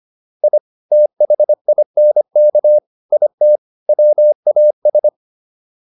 Morse code, heard as a keyed tone, about 25 wpm, 600 Hz. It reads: I THINK IT WAS
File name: WD304 25